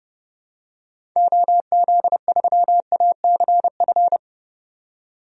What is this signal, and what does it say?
Morse code, heard as a keyed tone, about 30 wpm, 700 Hz. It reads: OZ3ACF
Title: DXCC-100 Call Signs - 6 Characters - 30wpm